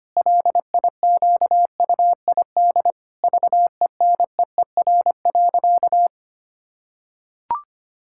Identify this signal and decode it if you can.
Morse code, heard as a keyed tone, about 25 wpm, 700 Hz. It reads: LIQUID VENEER.
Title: Pangram Challenge v2 - 25wpm